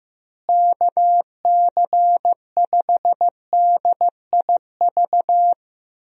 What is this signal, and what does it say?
Morse code, heard as a keyed tone, about 15 wpm, 700 Hz. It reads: KC5DIV
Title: USA Call Signs - 6 Characters - Repeated 2x - 15wpm